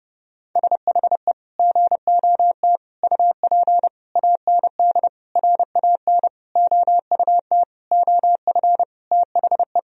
Morse code, heard as a keyed tone, about 30 wpm, 700 Hz. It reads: SHE GOT UP AND RAN OUT OF THE
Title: Sentences from Top 1000 Words - Review Part 3 - 30wpm